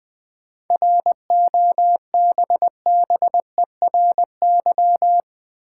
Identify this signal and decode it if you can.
Morse code, heard as a keyed tone, about 20 wpm, 690 Hz. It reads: ROBBERY
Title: 7 Letter Words - First 3 Letters Given - 20wpm